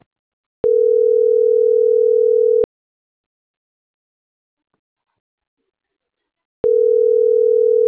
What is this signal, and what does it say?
Ringback tone (North America), 440 + 480 Hz.